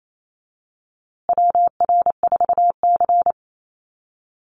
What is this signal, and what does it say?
Morse code, heard as a keyed tone, about 28 wpm, 700 Hz. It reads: WR4C
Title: USA Call Signs Repeated 3x 28wpm